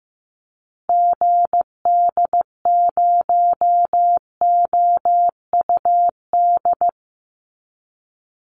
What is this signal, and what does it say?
Morse code, heard as a keyed tone, about 15 wpm, 700 Hz. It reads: GD0OUD
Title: Call Signs Repeated 3x - 15wpm